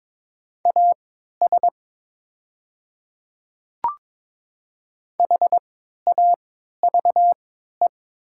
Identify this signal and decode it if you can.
Morse code, heard as a keyed tone, about 22 wpm, spaced as for 12 wpm, 700 Hz. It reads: AS HAVE
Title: 3x Character Spacing - Top 100 Words - 22wpm